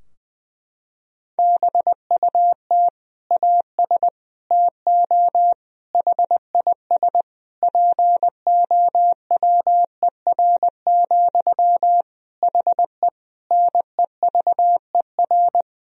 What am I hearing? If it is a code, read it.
Morse code, heard as a keyed tone, about 20 wpm, 700 Hz. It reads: BUT AS TO HIS POWER, HE NEVER
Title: Sentences from Top 1000 Words - Review Part 2 - 20wpm